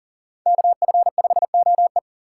Morse code, heard as a keyed tone, about 42 wpm, 700 Hz. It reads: KF5OE
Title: Speed Racing - USA 2x2 Call Signs 40wpm